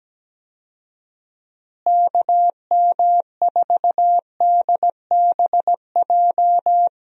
Morse code, heard as a keyed tone, about 17 wpm, 700 Hz. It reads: KM4DBJ
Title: USA Call Signs - 6 Characters - Repeated 2x - 17wpm